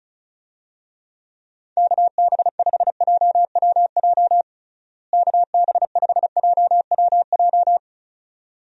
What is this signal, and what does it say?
Morse code, heard as a keyed tone, about 36 wpm, 700 Hz. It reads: KB5JWJ KB5JWJ
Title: USA Call Signs - 6 Characters - Repeated 2x - 35wpm